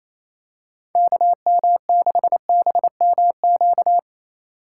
Morse code, heard as a keyed tone, about 28 wpm, 700 Hz. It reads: KM6BMQ
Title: USA Call Signs - 6 Characters - 3x Character Spacing - 28wpm